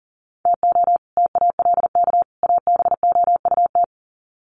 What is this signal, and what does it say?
Morse code, heard as a keyed tone, about 40 wpm, 700 Hz. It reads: TO TALK ABOUT
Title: Sets of 3 Words 40wpm